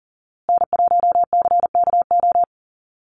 Morse code, heard as a keyed tone, about 40 wpm, 700 Hz. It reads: N1CKO
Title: USA Call Signs Repeated 3x 40wpm